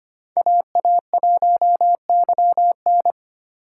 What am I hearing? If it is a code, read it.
Morse code, heard as a keyed tone, about 25 wpm, 700 Hz. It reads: AA1YN
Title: USA Call Signs - 5 Characters - 2x Character Spacing - 25wpm